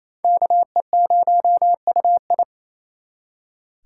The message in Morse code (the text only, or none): KE0UI